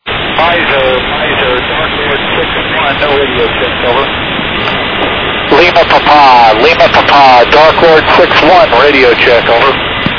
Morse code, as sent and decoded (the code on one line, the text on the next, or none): none